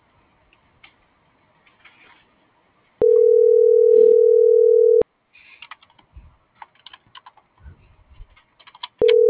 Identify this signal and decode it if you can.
Ringback tone (North America), 440 + 480 Hz.